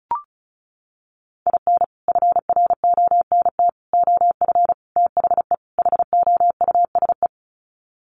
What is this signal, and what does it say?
Morse code, heard as a keyed tone, about 34 wpm, 700 Hz. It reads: IN FRONT OF THE HOUSE
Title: Sets of 5 Words Encore2 35wpm